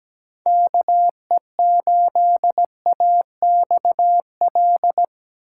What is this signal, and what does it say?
Morse code, heard as a keyed tone, about 17 wpm, 700 Hz. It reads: KE8AXL
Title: USA Call Signs - 3x Character Spacing - 17wpm